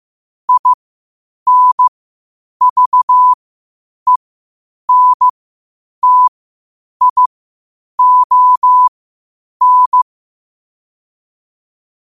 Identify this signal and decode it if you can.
Morse code, heard as a keyed tone, about 15 wpm, spaced as for 8 wpm, 1000 Hz. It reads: INVENTION
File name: qotdfiles / 8wpm